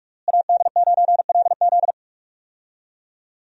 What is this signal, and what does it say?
Morse code, heard as a keyed tone, about 41 wpm, 700 Hz. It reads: AD9LZ
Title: USA Call Signs - 5 Characters - Repeated 3x - 45wpm